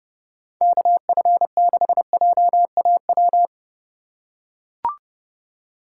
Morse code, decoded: KF6JAW

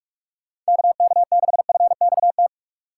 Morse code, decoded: KK6FXT

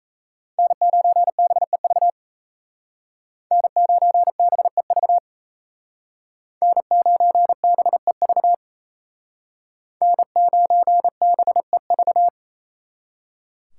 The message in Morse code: N9BEV N9BEV N9BEV N9BEV

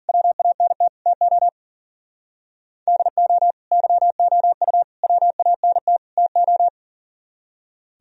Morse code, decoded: WANT TO DO YOU WANT TO

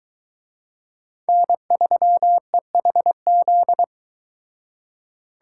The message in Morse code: N3EHZ

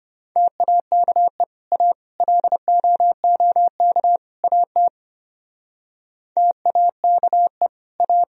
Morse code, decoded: TAKE A LOOK AT TAKE A

